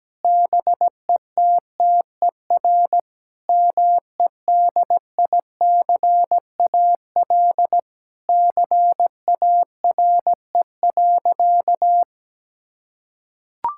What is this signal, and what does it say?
Morse code, heard as a keyed tone, about 17 wpm, 700 Hz. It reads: BETTER MEDICAL CARE.
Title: Sentences from Top 1000 Words - Review Part 5 - 17wpm